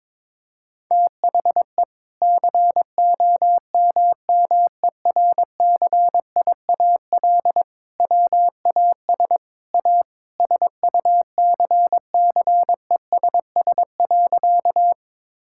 Morse code, heard as a keyed tone, about 22 wpm, 700 Hz. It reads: THE COMMERCIAL WAS A SUCCESS.
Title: Sentences from Top 1000 Words - Review Part 5 - 22wpm